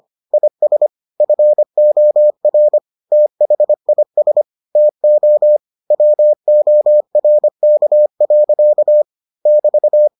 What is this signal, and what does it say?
Morse code, heard as a keyed tone, about 25 wpm, 600 Hz. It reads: IS FOR THIS TO WORK. <BT>